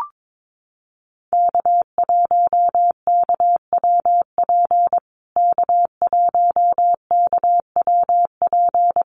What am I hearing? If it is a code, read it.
Morse code, heard as a keyed tone, about 22 wpm, 700 Hz. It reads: K1KWP K1KWP
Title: USA Call Signs Repeated 3x 22wpm